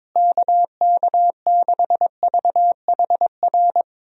Morse code, heard as a keyed tone, about 22 wpm, 700 Hz. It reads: KK6VHR
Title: USA Call Signs - 6 Characters - Repeated 3x - 22wpm